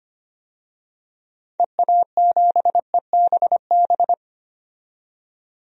Morse code, heard as a keyed tone, about 25 wpm, 700 Hz. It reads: EA7EBB
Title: DXCC-100 Call Signs - 6 Characters - 25wpm